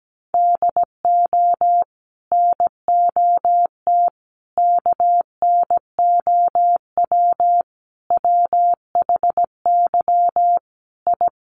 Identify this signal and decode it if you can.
Morse code, heard as a keyed tone, about 17 wpm, 700 Hz. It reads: DO NOT KNOW WHY I
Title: Sets of 5 Words Encore 17wpm